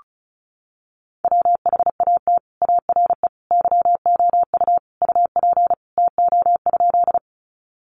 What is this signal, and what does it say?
Morse code, heard as a keyed tone, about 34 wpm, 700 Hz. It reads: WHAT ARE YOU UP TO?
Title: Sentences from Top 100 Words 35wpm